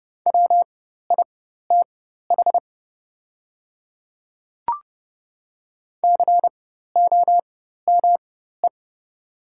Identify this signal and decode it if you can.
Morse code, heard as a keyed tone, about 29 wpm, spaced as for 14 wpm, 690 Hz. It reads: WITH COME
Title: 4x Character Spacing - Top 100 Words - 30wpm